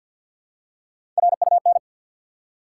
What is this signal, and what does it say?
Morse code, heard as a keyed tone, about 53 wpm, 700 Hz. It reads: RUN
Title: Three Letter Words - 50wpm